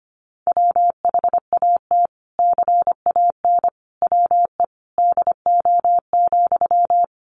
Morse code, heard as a keyed tone, about 25 wpm, 700 Hz. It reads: WHAT CAN WE DO,
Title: Sentences from Top 100 Words 25wpm